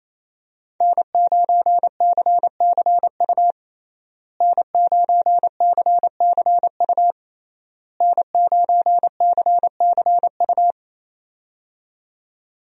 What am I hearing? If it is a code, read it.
Morse code, heard as a keyed tone, about 28 wpm, 700 Hz. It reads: N9CCU N9CCU N9CCU
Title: USA Call Signs - 5 Characters - Repeated 3x - 28wpm